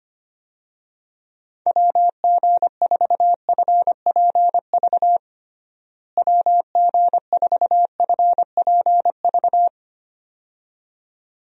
Morse code, heard as a keyed tone, about 25 wpm, 700 Hz. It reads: WG4FPV WG4FPV
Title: USA Call Signs - 6 Characters - Repeated 2x - 25wpm